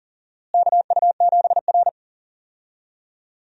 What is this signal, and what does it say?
Morse code, heard as a keyed tone, about 40 wpm, 700 Hz. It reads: KU7R
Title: USA Call Signs - 4 Characters - Repeated 2x - 40wpm